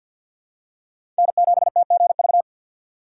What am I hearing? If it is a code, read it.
Morse code, heard as a keyed tone, about 53 wpm, 700 Hz. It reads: N7TGV
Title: USA Call Signs - 5 Characters - 8x Character Spacing - 50wpm